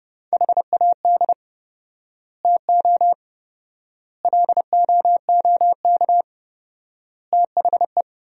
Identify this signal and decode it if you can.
Morse code, heard as a keyed tone, about 30 wpm, 700 Hz. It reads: HAD TO LOOK THE
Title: 4x Word-Spacing - Sentences from Top 100 Words - 30wpm